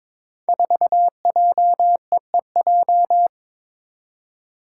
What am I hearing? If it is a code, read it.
Morse code, heard as a keyed tone, about 23 wpm, 700 Hz. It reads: 4JEEJ
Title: Five Challenging Letter-Number Combinations 22wpm